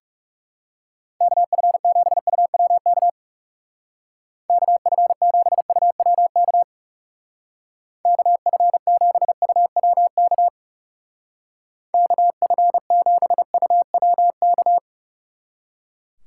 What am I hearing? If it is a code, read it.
Morse code, heard as a keyed tone, about 40 wpm, 700 Hz. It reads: KF7UWK KF7UWK KF7UWK KF7UWK